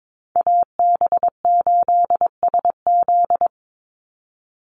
Morse code, heard as a keyed tone, about 22 wpm, 700 Hz. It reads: AB8SZ
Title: USA Call Signs 22wpm